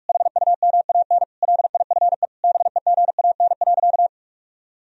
Morse code, heard as a keyed tone, about 41 wpm, 700 Hz. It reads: HUMAN LIFE BEGAN.
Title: Sentences from Top 1000 Words - Review Part 3 - 45wpm